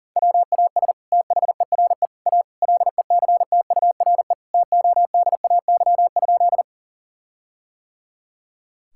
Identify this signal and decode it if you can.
Morse code, heard as a keyed tone, about 40 wpm, 700 Hz. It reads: WAS THERE A LECTURE TODAY?
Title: Sentences from Top 1600 Words 40wpm